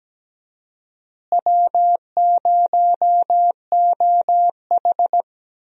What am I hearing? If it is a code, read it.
Morse code, heard as a keyed tone, about 17 wpm, 700 Hz. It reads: W0OH
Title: USA Call Signs - 4 Characters - Repeated 3x - 17wpm